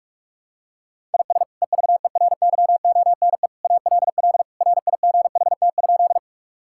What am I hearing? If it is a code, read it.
Morse code, heard as a keyed tone, about 46 wpm, 700 Hz. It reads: IS EVERYONE ALL RIGHT?
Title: Sentences from Top 1000 Words - Review Part 5 - 45wpm